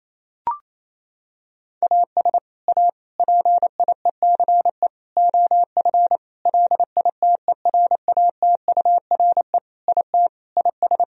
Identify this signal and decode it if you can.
Morse code, heard as a keyed tone, about 28 wpm, 700 Hz. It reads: AS A PIECE OF LITERATURE IT IS
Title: Sentences from Top 1000 Words - Review Part 3 - 28wpm